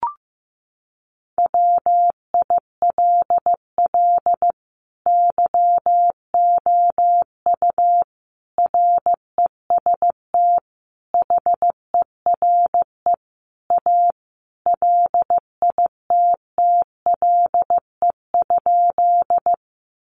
Morse code, heard as a keyed tone, about 15 wpm, 700 Hz. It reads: WILL YOU REST HERE A LITTLE?
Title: Sentences from Top 500 Words 15wpm